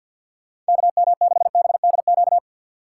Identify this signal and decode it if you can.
Morse code, heard as a keyed tone, about 52 wpm, 700 Hz. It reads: KM6BDQ
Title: USA Call Signs - 6 Characters - 3x Character Spacing - 50wpm